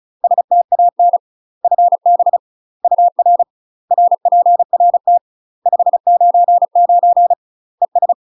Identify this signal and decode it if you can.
Morse code, heard as a keyed tone, about 35 wpm, 700 Hz. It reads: STAN FB UR RPRT 599 ES
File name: UK MASTER QSOs TRAINING / 35WPM / UK MASTER QSOs 0017